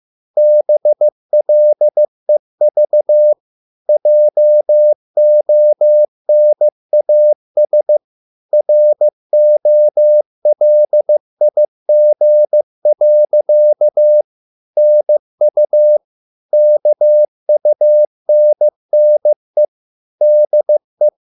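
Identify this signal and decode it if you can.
Morse code, heard as a keyed tone, about 15 wpm, 600 Hz. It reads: BLEV JONAS ROLIG. NU KUNNE DE
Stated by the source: Morse code at a slow 15 wpm from CW med Gnister.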